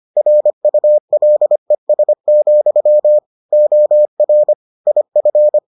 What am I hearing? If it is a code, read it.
Morse code, heard as a keyed tone, about 25 wpm, 600 Hz. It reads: RULES, OR IF